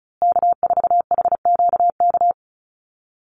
Morse code, heard as a keyed tone, about 34 wpm, 700 Hz. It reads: K4HQK
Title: Call Signs Repeated 2x - 35wpm